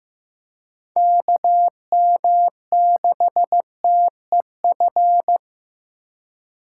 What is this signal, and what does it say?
Morse code, heard as a keyed tone, about 15 wpm, 700 Hz. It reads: KM6TEF
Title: USA Call Signs - 6 Characters - 8x Character Spacing - 15wpm